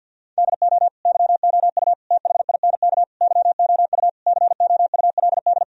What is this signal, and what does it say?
Morse code, heard as a keyed tone, about 51 wpm, 700 Hz. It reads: DO YOU THINK YOU COULD
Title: Sentences from Top 1000 Words - Review Part 1 - 50wpm